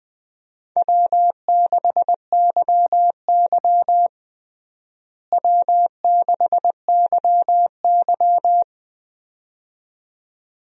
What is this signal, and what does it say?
Morse code, heard as a keyed tone, about 21 wpm, 700 Hz. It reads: W6YY W6YY
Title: USA Call Signs - 4 Characters - Repeated 2x - 20wpm